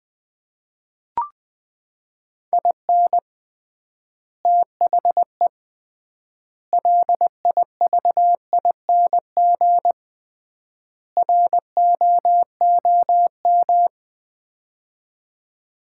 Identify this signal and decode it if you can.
Morse code, heard as a keyed tone, about 20 wpm, 700 Hz. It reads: IN THE LIVING ROOM
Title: 3x Word-Spacing - Sets of 4 Words - 20wpm